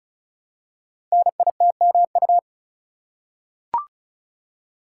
Morse code, heard as a keyed tone, about 36 wpm, 700 Hz. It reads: NITMU
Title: Code Groups - 5 Letter:Numbers - 35wpm